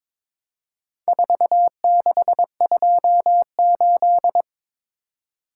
Morse code, heard as a keyed tone, about 22 wpm, 700 Hz. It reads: 4628
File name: practice / 22wpm / Four Letter-Number Combinations - 22wpm